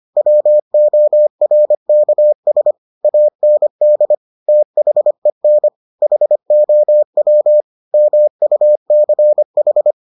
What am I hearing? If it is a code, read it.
Morse code, heard as a keyed tone, about 25 wpm, 600 Hz. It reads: WORKS AND THEN HOW MUCH